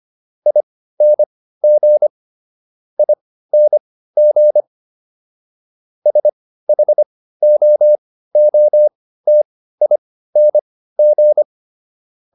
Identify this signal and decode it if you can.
Morse code, heard as a keyed tone, about 25 wpm, spaced as for 15 wpm, 600 Hz. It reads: ING ING SHOOTING